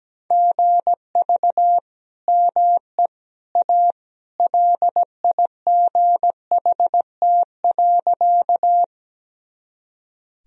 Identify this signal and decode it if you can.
Morse code, heard as a keyed tone, about 17 wpm, 700 Hz. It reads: GV ME A LIGHT.
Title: Sentences from Top 200 Words and Abbreviations 17wpm